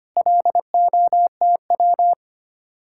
Morse code, heard as a keyed tone, about 32 wpm, 700 Hz. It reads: LOTW